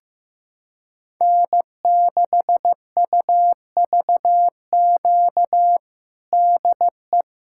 Morse code, heard as a keyed tone, about 15 wpm, 700 Hz. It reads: N6UVQ DE
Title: USA Callsign DE Callsign - 5 Characters - 15wpm